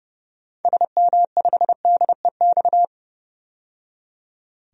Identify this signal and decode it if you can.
Morse code, heard as a keyed tone, about 29 wpm, 700 Hz. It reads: SM5DEX
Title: DXCC-100 Call Signs - 6 Characters - 30wpm